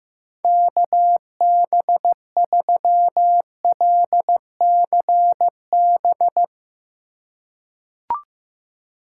Morse code, heard as a keyed tone, about 15 wpm, 700 Hz. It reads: KB3LCB
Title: USA Call Signs - 6 Characters - 3x Character Spacing - 15wpm